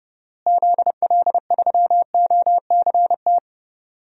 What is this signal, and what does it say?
Morse code, heard as a keyed tone, about 31 wpm, 700 Hz. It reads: ZL3OCT
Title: DXCC-100 Call Signs - 6 Characters - Repeated 3x - 30wpm